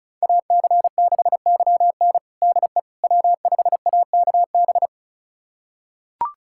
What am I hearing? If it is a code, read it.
Morse code, heard as a keyed tone, about 34 wpm, 700 Hz. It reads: AC6YN DE W5AKB
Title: USA Callsign DE Callsign - 5 Characters - 35wpm